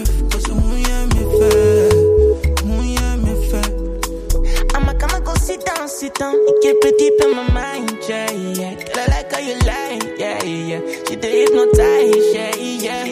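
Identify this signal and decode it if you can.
Ringback tone (Europe, 425 Hz), with noise.
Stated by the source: Welcome melody that callers hear before the call is answered, from the O2 Czech Republic O2 Knihovna service.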